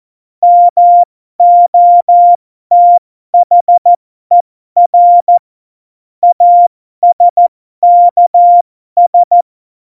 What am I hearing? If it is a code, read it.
Morse code, heard as a keyed tone, about 14 wpm, spaced as for 12 wpm, 700 Hz. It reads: MOTHER ASKS
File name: Morsekurs / Witze